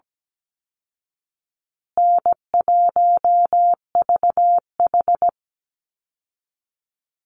Morse code, heard as a keyed tone, about 17 wpm, 700 Hz. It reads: N1VH
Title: Easy Call Signs Repeated 3x - 17wpm